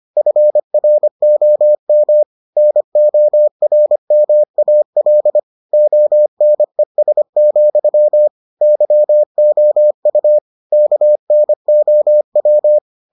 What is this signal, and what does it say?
Morse code, heard as a keyed tone, about 25 wpm, 600 Hz. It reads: FROM NORMAL ONES, YOU KNOW